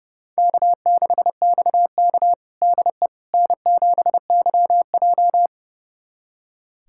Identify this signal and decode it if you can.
Morse code, heard as a keyed tone, about 30 wpm, 700 Hz. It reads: K6XK DE N7YJ
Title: USA Callsign DE Callsign - 4 Characters - 30wpm